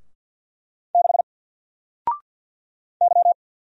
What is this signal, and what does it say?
Morse code, heard as a keyed tone, about 50 wpm, 700 Hz. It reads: 6 Y
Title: Single Letter-Number 50wpm